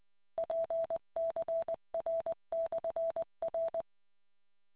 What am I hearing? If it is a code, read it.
Morse code, heard as a keyed tone, about 20 wpm, 660 Hz. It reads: PCR/R